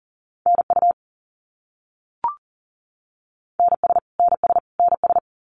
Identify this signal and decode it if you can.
Morse code, heard as a keyed tone, about 40 wpm, 700 Hz. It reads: NU NS NS NS
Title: Canadian Provinces and Territories - Repeated 3x - 40wpm